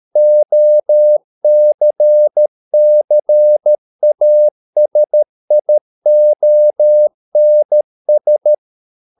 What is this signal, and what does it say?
Morse code, heard as a keyed tone, about 13 wpm, 600 Hz. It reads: OCCASIONS